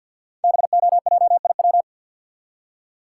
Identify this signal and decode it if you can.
Morse code, heard as a keyed tone, about 50 wpm, 700 Hz. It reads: BOJIW
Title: Code Groups - 5 Letters - 50wpm